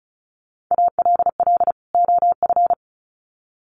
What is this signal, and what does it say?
Morse code, heard as a keyed tone, about 34 wpm, 700 Hz. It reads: ALL OF
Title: Sets of 2 Words 35wpm